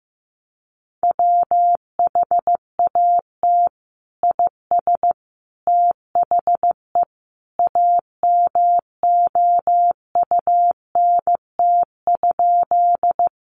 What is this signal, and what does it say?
Morse code, heard as a keyed tone, about 15 wpm, 700 Hz. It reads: WHAT IS THE AMOUNT?
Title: Sentences from Top 700 Words 15wpm